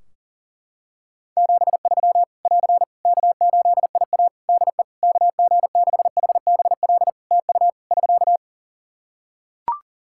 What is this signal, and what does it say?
Morse code, heard as a keyed tone, about 37 wpm, 700 Hz. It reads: 73 <AR> K8IA DE KG6HBL TU <SK>
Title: Practice QSO - Exchange 4 - Ending - 40wpm